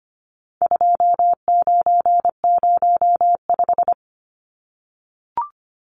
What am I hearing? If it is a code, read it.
Morse code, heard as a keyed tone, about 25 wpm, 700 Hz. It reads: 2905
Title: Numbers - 4-Digit 25wpm